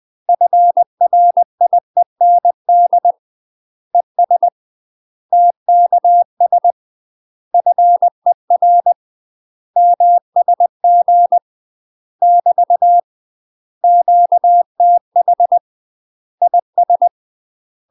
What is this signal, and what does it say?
Morse code, heard as a keyed tone, about 20 wpm, 700 Hz. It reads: FRIEND ES TKS FER MSG <BT> QTH IS
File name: UK MASTER QSOs TRAINING / 20WPM / UK MASTER QSOs 0024